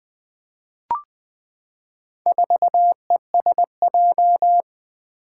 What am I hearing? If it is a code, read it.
Morse code, heard as a keyed tone, about 20 wpm, 700 Hz. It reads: 4ESJ
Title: Four Challenging Letter-Number Combinations 20wpm